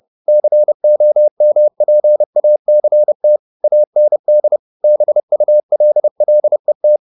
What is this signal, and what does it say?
Morse code, heard as a keyed tone, about 30 wpm, 600 Hz. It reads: COMPACT AND BULLET
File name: the old man and the sea / the old man and the sea 1 30wpm